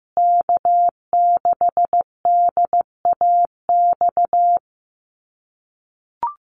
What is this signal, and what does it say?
Morse code, heard as a keyed tone, about 15 wpm, 700 Hz. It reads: K6DAX
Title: USA Call Signs Repeated 3x 15wpm